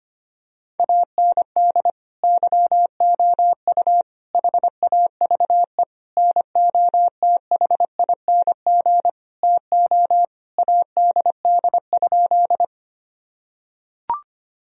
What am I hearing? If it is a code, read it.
Morse code, heard as a keyed tone, about 25 wpm, 700 Hz. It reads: AND YOU HAVE NOTHING TO ADD?
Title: Sentences from Top 1000 Words - Review Part 1 - 25wpm